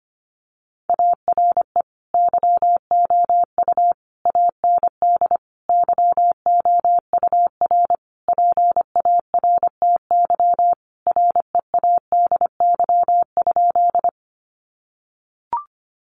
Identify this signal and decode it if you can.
Morse code, heard as a keyed tone, about 25 wpm, 700 Hz. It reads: ARE YOU AND YOUR PARTY READY?
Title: Sentences from Top 1400 Words 25wpm